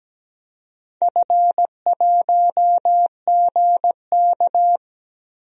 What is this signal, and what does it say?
Morse code, heard as a keyed tone, about 17 wpm, 700 Hz. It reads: F1GK
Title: DXCC-100 Call Signs - 4 Characters - Repeated 3x - 17wpm